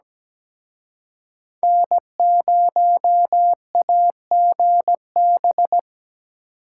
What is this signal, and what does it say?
Morse code, heard as a keyed tone, about 20 wpm, 700 Hz. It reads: N0AGB
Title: Speed Racing - USA 1x3 Call Signs 17wpm